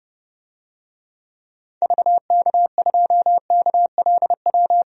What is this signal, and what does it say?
Morse code, heard as a keyed tone, about 31 wpm, 700 Hz. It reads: VK2KLW
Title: DXCC-100 Call Signs - 6 Characters - 30wpm